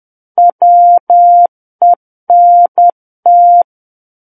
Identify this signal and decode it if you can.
Morse code, heard as a keyed tone, about 10 wpm, 700 Hz. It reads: WENT
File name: Book One / 01-Chapter 1 - 10 WPM